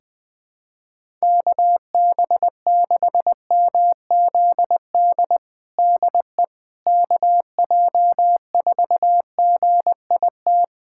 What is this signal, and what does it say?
Morse code, heard as a keyed tone, about 20 wpm, 700 Hz. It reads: KB6MZD DE KJ4GIT
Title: USA Callsign DE Callsign - 6 Characters - 20wpm